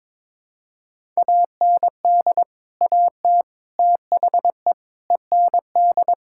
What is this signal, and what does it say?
Morse code, heard as a keyed tone, about 22 wpm, 700 Hz. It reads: AND AT THE END
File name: practice / 22wpm / Build A Phrase - 5-Word Phrases - 22wpm